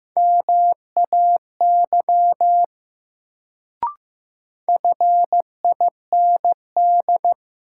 Morse code, heard as a keyed tone, about 15 wpm, 700 Hz. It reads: MAY FIND
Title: Sentences from Top 100 Words - New Word Drill - 15wpm